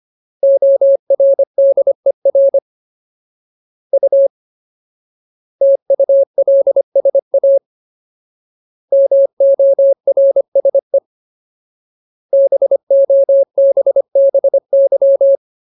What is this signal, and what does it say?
Morse code, heard as a keyed tone, about 25 wpm, 550 Hz. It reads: ORDER U TULSA MORSE BOBBY
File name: Ultimate Word List / 25WPM Ultimate